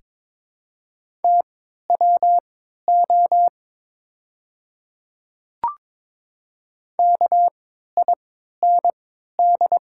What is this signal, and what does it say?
Morse code, heard as a keyed tone, about 23 wpm, spaced as for 13 wpm, 690 Hz. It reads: TWO KIND
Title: 3x Character Spacing - Top 300 Words - 22wpm